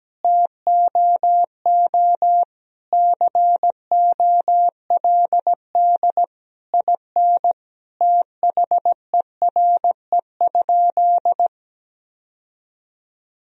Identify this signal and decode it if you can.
Morse code, heard as a keyed tone, about 17 wpm, 700 Hz. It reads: TOO COLD IN THERE?
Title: Sentences from Top 1000 Words - Review Part 4 - 17wpm